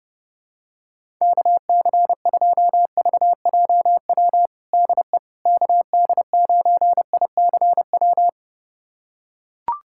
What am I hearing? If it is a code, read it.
Morse code, heard as a keyed tone, about 30 wpm, 700 Hz. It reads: KC2VJW DE KD9ICW
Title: USA Callsign DE Callsign 30wpm